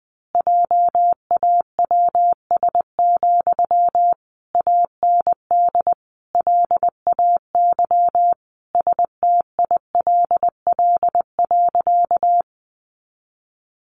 Morse code, heard as a keyed tone, about 20 wpm, 700 Hz. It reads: JAWS, AND LAY STILL.
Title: A Sound Of Thunder 20wpm